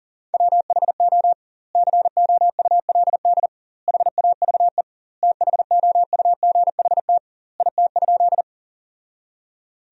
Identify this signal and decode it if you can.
Morse code, heard as a keyed tone, about 40 wpm, 700 Hz. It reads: WHO COULD HAVE THOUGHT IT?
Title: 2x Word-Spacing - Sentences from Top 300 Words - 40wpm